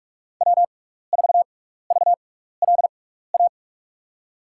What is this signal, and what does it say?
Morse code, heard as a keyed tone, about 43 wpm, spaced as for 15 wpm, 700 Hz. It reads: W4VLA
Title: USA Call Signs - 5 Characters - 6x Character Spacing - 45wpm